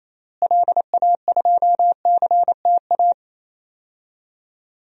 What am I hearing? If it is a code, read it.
Morse code, heard as a keyed tone, about 28 wpm, 700 Hz. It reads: LA2CTA
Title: DXCC-100 Call Signs - 6 Characters - Repeated 2x - 28wpm